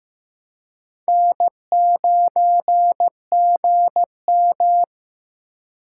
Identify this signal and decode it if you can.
Morse code, heard as a keyed tone, about 18 wpm, 700 Hz. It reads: N9GM
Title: Speed Racing - USA 1x2 Call Signs 15wpm